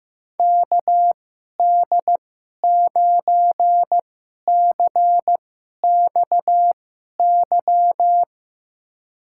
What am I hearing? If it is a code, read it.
Morse code, heard as a keyed tone, about 16 wpm, spaced as for 11 wpm, 700 Hz. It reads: KD9CXY
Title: USA Call Signs - 6 Characters - 2x Character Spacing - 15wpm